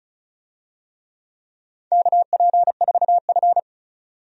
Morse code, heard as a keyed tone, about 36 wpm, 700 Hz. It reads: KP4F
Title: DXCC-100 Call Signs - 4 Characters - 35wpm